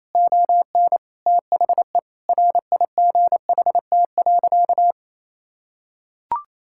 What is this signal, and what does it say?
Morse code, heard as a keyed tone, about 28 wpm, 700 Hz. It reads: ON THE RIGHT.
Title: Sentences from Top 1600 Words 28wpm